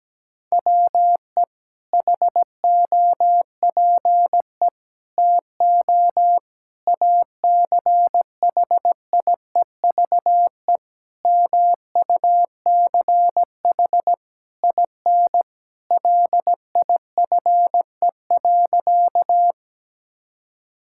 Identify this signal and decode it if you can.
Morse code, heard as a keyed tone, about 17 wpm, 700 Hz. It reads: WE HOPE TO ACHIEVE MUCH IN LIFE.
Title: Sentences from Top 1000 Words - Review Part 6 - 17wpm